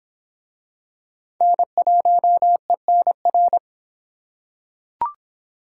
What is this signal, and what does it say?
Morse code, heard as a keyed tone, about 25 wpm, 700 Hz. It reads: N1ENR